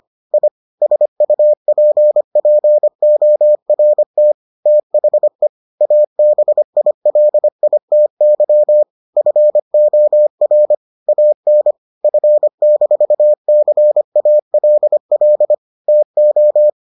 Morse code, heard as a keyed tone, about 25 wpm, 600 Hz. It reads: I SUPPORT THE ABILITY FOR AN F-CALL TO